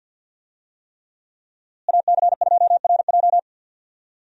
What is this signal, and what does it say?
Morse code, heard as a keyed tone, about 52 wpm, 700 Hz. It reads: AC2RJ
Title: USA Call Signs - 5 Characters - Repeated 3x - 50wpm